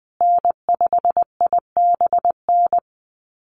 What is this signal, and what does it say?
Morse code, heard as a keyed tone, about 20 wpm, 700 Hz. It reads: N5IBN
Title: USA Call Signs Repeated 2x 20wpm